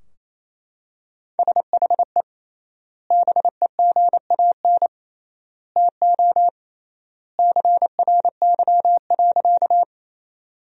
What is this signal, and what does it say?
Morse code, heard as a keyed tone, about 28 wpm, 700 Hz. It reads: SHE BEGAN TO CRY.
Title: 3x Word-Spacing - Sentences from Top 500 Words - 28wpm